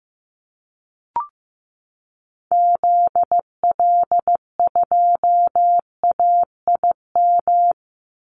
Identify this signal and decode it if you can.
Morse code, heard as a keyed tone, about 15 wpm, 700 Hz. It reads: ZL2AIM
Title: Call Signs Repeated 2x - 15wpm